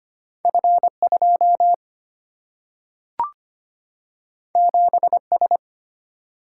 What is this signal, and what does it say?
Morse code, heard as a keyed tone, about 25 wpm, 700 Hz. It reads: F2 7S
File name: Two Letter-Number Combinations - 25wpm